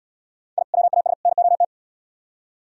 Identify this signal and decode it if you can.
Morse code, heard as a keyed tone, about 76 wpm, 700 Hz. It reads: I WAS NOT